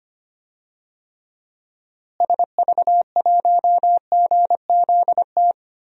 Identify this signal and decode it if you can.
Morse code, heard as a keyed tone, about 25 wpm, 700 Hz. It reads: SV1GZT
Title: DXCC-100 Call Signs - 6 Characters - Repeated 3x - 25wpm